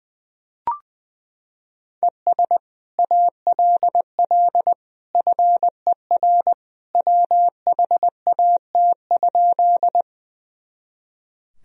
Morse code, heard as a keyed tone, about 21 wpm, 700 Hz. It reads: ES ALL FER WHAT?
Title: Sentences from Top 100 Words and Abbreviations 20wpm